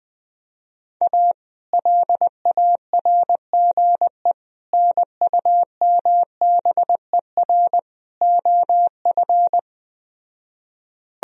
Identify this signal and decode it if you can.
Morse code, heard as a keyed tone, about 20 wpm, 700 Hz. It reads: A LARGE NUMBER OF
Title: Speed-Racing - Sets of 4 Words 17wpm